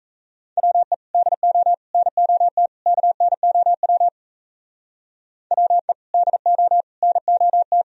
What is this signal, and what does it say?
Morse code, heard as a keyed tone, about 33 wpm, 700 Hz. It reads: WE DO NOT KNOW WE DO NOT
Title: Speed-Racing - Sets of 4 Words 28wpm